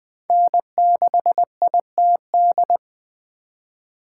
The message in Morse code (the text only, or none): N6ITD